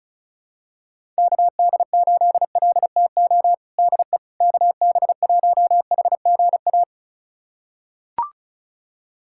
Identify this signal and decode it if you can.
Morse code, heard as a keyed tone, about 36 wpm, 700 Hz. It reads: KD8LTO DE KB1HGA
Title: USA Callsign DE Callsign - 6 Characters - 35wpm